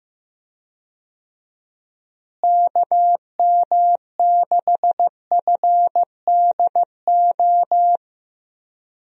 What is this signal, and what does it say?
Morse code, heard as a keyed tone, about 15 wpm, 700 Hz. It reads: KM6FDO
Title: USA Call Signs - 6 Characters - Repeated 2x - 15wpm